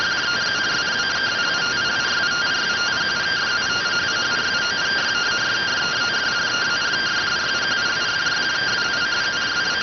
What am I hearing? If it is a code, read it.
Two-tone teleprinter signal, about 170 Hz shift.